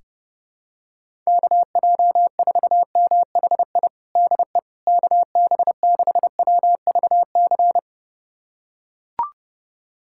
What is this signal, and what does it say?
Morse code, heard as a keyed tone, about 30 wpm, 700 Hz. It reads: KJ4MHI DE KB6WVC
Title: USA Callsign DE Callsign - 6 Characters - 30wpm